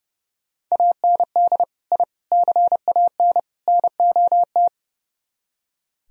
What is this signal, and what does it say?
Morse code, heard as a keyed tone, about 35 wpm, 700 Hz. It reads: AND I CAN NOT
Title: Speed-Racing - Sets of 4 Words 30wpm